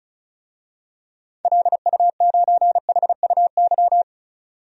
Morse code, heard as a keyed tone, about 34 wpm, 700 Hz. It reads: LU9HUY
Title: DXCC-100 Call Signs - 6 Characters - 35wpm